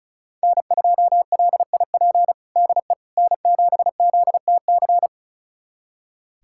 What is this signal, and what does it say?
Morse code, heard as a keyed tone, about 34 wpm, 700 Hz. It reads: N2LIP DE N7ZTC
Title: USA Callsign DE Callsign - 5 Characters - 35wpm